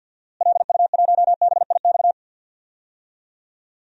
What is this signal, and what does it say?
Morse code, heard as a keyed tone, about 52 wpm, 700 Hz. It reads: LU1BIX
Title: DXCC-100 Call Signs - 6 Characters - Repeated 3x - 50wpm